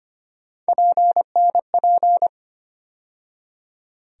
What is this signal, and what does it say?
Morse code, heard as a keyed tone, about 25 wpm, 700 Hz. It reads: PNP
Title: USA Call Sign Postfixes - 3 Letters 25wpm